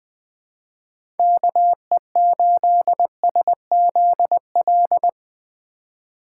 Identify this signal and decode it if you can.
Morse code, heard as a keyed tone, about 21 wpm, 700 Hz. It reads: KE8SZL
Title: 2023 POTA Activator Call Signs - 20wpm